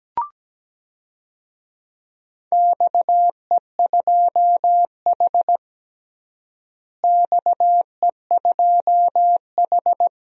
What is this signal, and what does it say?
Morse code, heard as a keyed tone, about 17 wpm, 700 Hz. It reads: XE2H XE2H
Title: DXCC-100 Call Signs - 4 Characters - Repeated 2x - 17wpm